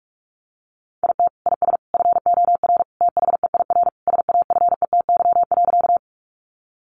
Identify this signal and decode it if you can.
Morse code, heard as a keyed tone, about 44 wpm, 700 Hz. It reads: IT IS FOR THEIR SAFETY.
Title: Sentences from Top 700 Words 45wpm